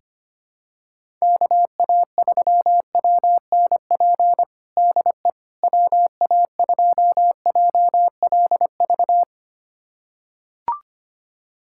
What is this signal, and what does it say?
Morse code, heard as a keyed tone, about 25 wpm, 700 Hz. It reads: KA3WNP DE WA2JLV
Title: USA Callsign DE Callsign - 6 Characters - 25wpm